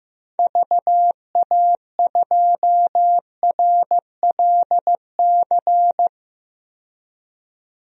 Morse code, heard as a keyed tone, about 15 wpm, 700 Hz. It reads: VA2RLC
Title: DXCC-100 Call Signs - 6 Characters - 15wpm